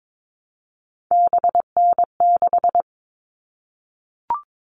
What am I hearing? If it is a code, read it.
Morse code, heard as a keyed tone, about 22 wpm, 700 Hz. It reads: BN6
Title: All Cut Numbers - 3-Digit 22wpm